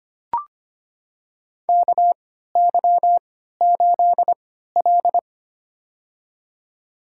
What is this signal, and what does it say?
Morse code, heard as a keyed tone, about 25 wpm, spaced as for 14 wpm, 700 Hz. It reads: KY8L
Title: USA Call Signs - 4 Characters - 3x Character Spacing - 25wpm